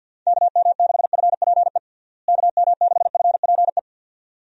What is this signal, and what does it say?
Morse code, heard as a keyed tone, about 50 wpm, 700 Hz. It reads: KM6FPE KM6FPE
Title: USA Call Signs - 6 Characters - Repeated 2x - 50wpm